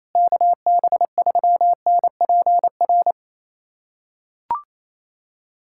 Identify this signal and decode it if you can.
Morse code, heard as a keyed tone, about 28 wpm, 700 Hz. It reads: KB3NPR
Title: USA Callsign DE Callsign - 6 Characters - 28wpm